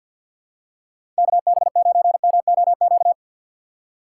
Morse code, heard as a keyed tone, about 53 wpm, 700 Hz. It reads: KB9MOQ
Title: USA Call Signs - 6 Characters - 3x Character Spacing - 50wpm